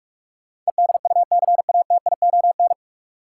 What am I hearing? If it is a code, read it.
Morse code, heard as a keyed tone, about 46 wpm, 700 Hz. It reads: EDUCATION
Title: Sentences from Top 1100 Words - New Word Drill Alt - 45wpm